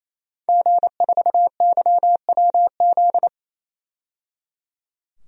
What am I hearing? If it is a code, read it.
Morse code, heard as a keyed tone, about 28 wpm, 700 Hz. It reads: G4YWZ
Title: DXCC-100 Call Signs - 5 Characters - 28wpm